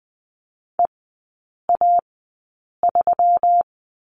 Morse code, heard as a keyed tone, about 20 wpm, spaced as for 8 wpm, 700 Hz. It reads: EA3
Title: 138 Lesson 23e - Letter f - Three Characters 20wpm